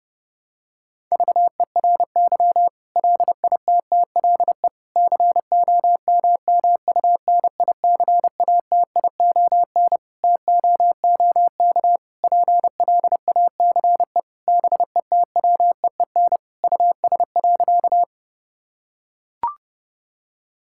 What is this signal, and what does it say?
Morse code, heard as a keyed tone, about 29 wpm, 700 Hz. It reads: VERY LITTLE COMMUNICATION TOOK PLACE BETWEEN US.
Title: Sentences from Top 1000 Words - Review Part 6 - 30wpm